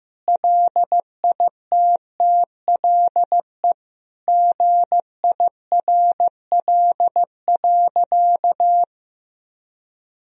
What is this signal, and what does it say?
Morse code, heard as a keyed tone, about 16 wpm, 700 Hz. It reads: LITTLE GIRL.
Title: Sentences from Top 400 Words and Abbreviations 15wpm